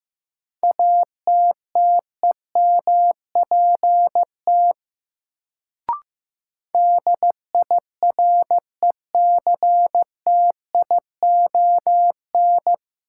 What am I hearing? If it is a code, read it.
Morse code, heard as a keyed tone, about 15 wpm, 700 Hz. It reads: ATTEMPT DIRECTION
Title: Sets of 4 Words - New Word Drill - 15wpm